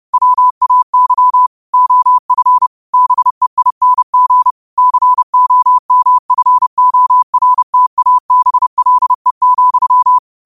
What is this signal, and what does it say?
Morse code, heard as a keyed tone, about 30 wpm, 1000 Hz. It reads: WAY OF BEING COMFORTABLE,
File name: qotdfiles / 30wpm